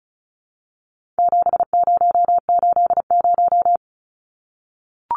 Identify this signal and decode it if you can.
Morse code, heard as a keyed tone, about 34 wpm, 700 Hz. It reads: 7080
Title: Numbers - 4-Digit 35wpm